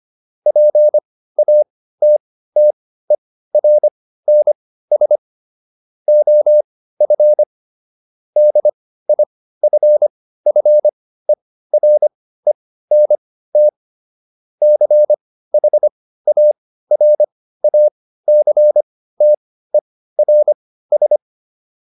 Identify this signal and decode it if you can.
Morse code, heard as a keyed tone, about 25 wpm, spaced as for 15 wpm, 600 Hz. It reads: PATTERNS OF DIFFERENT CHARACTERS